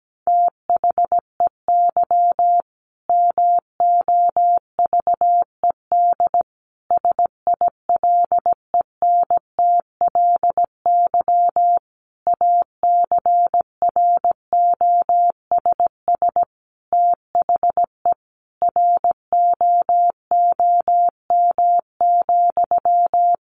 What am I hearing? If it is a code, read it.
Morse code, heard as a keyed tone, about 17 wpm, 700 Hz. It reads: THEY MOVED SILENTLY ACROSS THE ROOM,
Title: A Sound Of Thunder 17wpm